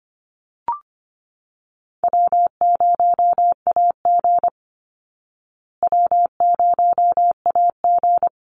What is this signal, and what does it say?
Morse code, heard as a keyed tone, about 25 wpm, 700 Hz. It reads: W0AG W0AG
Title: Easy Call Signs Repeated 3x - 25wpm